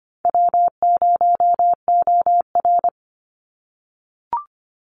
Morse code, heard as a keyed tone, about 25 wpm, 700 Hz. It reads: W0OR
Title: Easy Call Signs - 25wpm